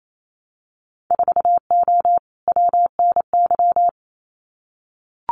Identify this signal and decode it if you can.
Morse code, heard as a keyed tone, about 28 wpm, 700 Hz. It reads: 4O WNY
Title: Winter Field Day Exchanges Encore 28wpm